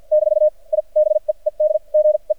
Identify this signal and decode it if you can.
Morse code, heard as a keyed tone, about 48 wpm, spaced as for 36 wpm, 610 Hz. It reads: - I BE EDGE
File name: archived / CRSnet